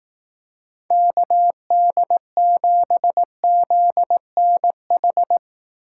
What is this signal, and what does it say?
Morse code, heard as a keyed tone, about 18 wpm, 700 Hz. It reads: KD7ZNH
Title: Speed Racing - USA 2x3 Call Signs 15wpm